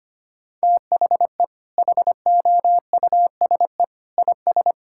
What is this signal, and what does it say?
Morse code, heard as a keyed tone, about 25 wpm, 700 Hz. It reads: THE HOUSE IS
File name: practice / 25wpm / Sentences with 2-Word Phrases - 25wpm